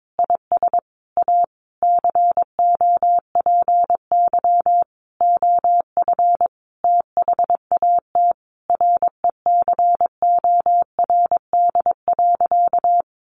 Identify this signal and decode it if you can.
Morse code, heard as a keyed tone, about 22 wpm, 700 Hz. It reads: IS A COPY OF THAT RECORD.
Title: Sentences from Top 1400 Words 22wpm